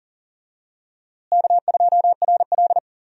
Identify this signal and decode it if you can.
Morse code, heard as a keyed tone, about 40 wpm, 700 Hz. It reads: K2RL
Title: USA Call Signs - 4 Characters - Repeated 2x - 40wpm